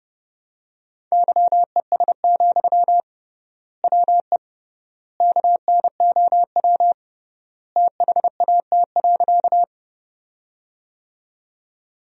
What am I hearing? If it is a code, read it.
Morse code, heard as a keyed tone, about 30 wpm, 700 Hz. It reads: YES, WE KNOW THAT.
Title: 3x Word-Spacing - Sentences from Top 500 Words - 30wpm